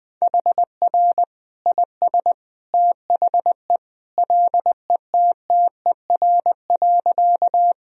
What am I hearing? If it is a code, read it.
Morse code, heard as a keyed tone, about 20 wpm, 700 Hz. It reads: HR IS THE LETTER.
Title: Sentences from Top 300 Words and Abbreviations 20wpm